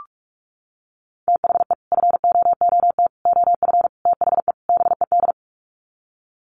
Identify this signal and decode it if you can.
Morse code, heard as a keyed tone, about 44 wpm, 700 Hz. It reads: THE FOOT OF THE BED
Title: Sets of 5 Words Encore2 45wpm